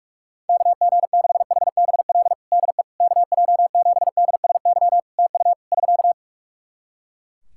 Morse code, heard as a keyed tone, about 43 wpm, 700 Hz. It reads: KG6HBL DE KJ7DSY TU <SK>
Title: K1USN SST QSO Practice 45wpm